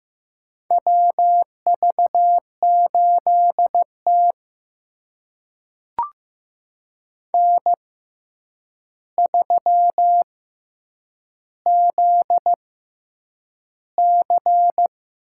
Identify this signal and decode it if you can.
Morse code, heard as a keyed tone, about 15 wpm, spaced as for 8 wpm, 700 Hz. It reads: WV8T N3ZC
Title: USA Call Signs - 4 Characters - 6x Character Spacing - 15wpm